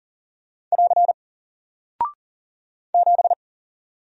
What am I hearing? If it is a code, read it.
Morse code, heard as a keyed tone, about 40 wpm, 700 Hz. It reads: <AR> 7
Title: Single Character (Letter:Number:Punctuation:Prosign) Alt 40wpm